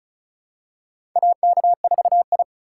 Morse code, heard as a keyed tone, about 36 wpm, 700 Hz. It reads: AK4I
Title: USA Call Signs - 3x Character Spacing - 35wpm